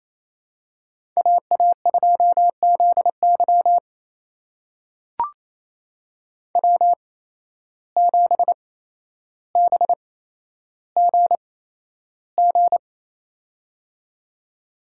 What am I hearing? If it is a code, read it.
Morse code, heard as a keyed tone, about 27 wpm, spaced as for 12 wpm, 700 Hz. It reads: AA2ZY W7BGG
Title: USA Call Signs - 5 Characters - 8x Character Spacing - 28wpm